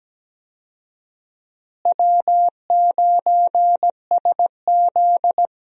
Morse code, heard as a keyed tone, about 17 wpm, 700 Hz. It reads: W9SZ
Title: USA Call Signs - 4 Characters - Repeated 3x - 17wpm